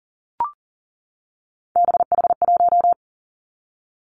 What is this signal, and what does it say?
Morse code, heard as a keyed tone, about 42 wpm, 700 Hz. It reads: BH1
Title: Three Challenging Letter-Number Combinations 40wpm